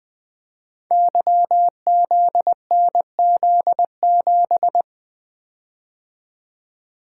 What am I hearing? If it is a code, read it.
Morse code, heard as a keyed tone, about 20 wpm, 700 Hz. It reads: YZNZ7
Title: Code Groups - 5 Letter:Numbers - 20wpm